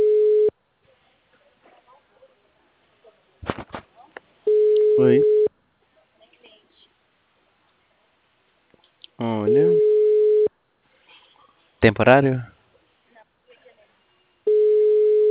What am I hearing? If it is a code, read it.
Ringback tone (Europe, 425 Hz), with speech.
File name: out- -2010- - - .8607